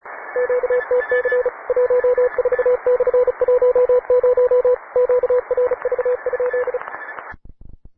Morse code, heard as a keyed tone, about 35 wpm, 500 Hz, with radio noise. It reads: QTC 14/10 QRV?